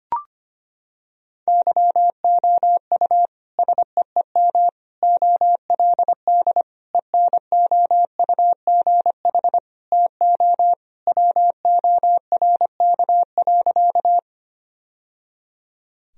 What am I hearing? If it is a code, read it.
Morse code, heard as a keyed tone, about 25 wpm, 700 Hz. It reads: YOU SEEM OLD ENOUGH TO WORK.
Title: Sentences from Top 1000 Words - Review Part 1 - 25wpm